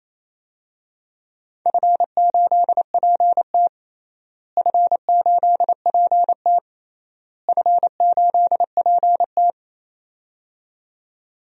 Morse code, heard as a keyed tone, about 28 wpm, 700 Hz. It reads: F8PT F8PT F8PT
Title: DXCC-100 Call Signs - 4 Characters - Repeated 3x - 28wpm